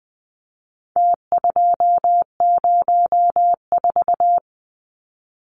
Morse code, heard as a keyed tone, about 20 wpm, 700 Hz. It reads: T204
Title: Numbers - Contest Style - 4-Digit 20wpm